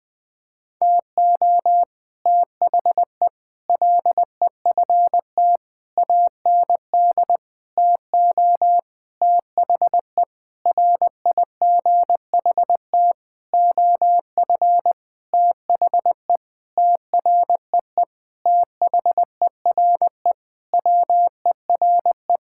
Morse code, heard as a keyed tone, about 21 wpm, 700 Hz. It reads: TO THE LEFT AND TO THE RIGHT OF THE TREE THERE WERE
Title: Sentences with 9-Word Phrases - Spoken Beginning - 20wpm